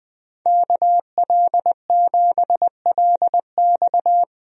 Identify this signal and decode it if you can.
Morse code, heard as a keyed tone, about 21 wpm, 700 Hz. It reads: KL7LX
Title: DXCC-100 Call Signs - 5 Characters - 20wpm